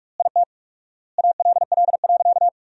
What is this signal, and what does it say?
Morse code, heard as a keyed tone, about 46 wpm, 700 Hz. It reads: IT ALL.
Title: 4x Word-Spacing - Sentences from Top 300 Words - 45wpm